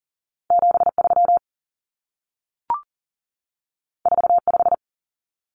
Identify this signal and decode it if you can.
Morse code, heard as a keyed tone, about 40 wpm, 700 Hz. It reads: 73 45
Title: Numbers - 2-Digit 40wpm